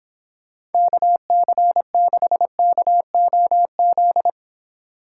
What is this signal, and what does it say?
Morse code, heard as a keyed tone, about 29 wpm, 700 Hz. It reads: KC6KOZ